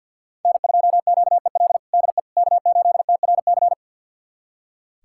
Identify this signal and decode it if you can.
Morse code, heard as a keyed tone, about 46 wpm, 700 Hz. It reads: N2QEL DE K8TRC
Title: USA Callsign DE Callsign - 5 Characters - 50wpm